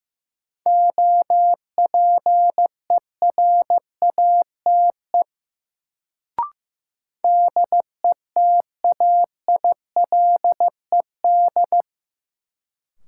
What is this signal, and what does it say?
Morse code, heard as a keyed tone, about 15 wpm, 700 Hz. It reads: OPERATE DETAILED
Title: Sentences from Top 1500 Words - New Word Drill Alt - 15wpm